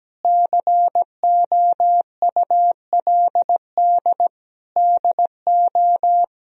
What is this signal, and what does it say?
Morse code, heard as a keyed tone, about 17 wpm, 700 Hz. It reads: COULD DO
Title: Sentences from Top 1000 Words - Review Part 4 - 17wpm